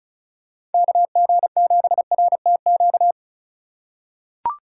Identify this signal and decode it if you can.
Morse code, heard as a keyed tone, about 36 wpm, 690 Hz. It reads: KG7RTQ
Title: USA Call Signs - 6 Characters - 3x Character Spacing - 35wpm